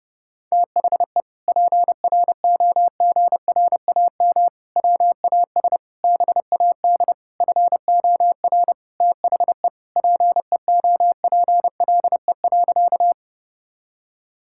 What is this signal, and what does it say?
Morse code, heard as a keyed tone, about 30 wpm, 700 Hz. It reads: THE PROGRAM WAS BAD FOR THE PEOPLE.
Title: Sentences from Top 1000 Words - Review Part 5 - 30wpm